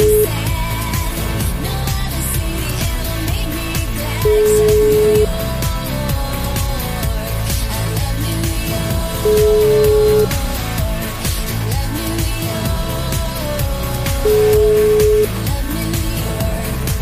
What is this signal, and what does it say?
Ringback tone (Europe, 425 Hz), with noise.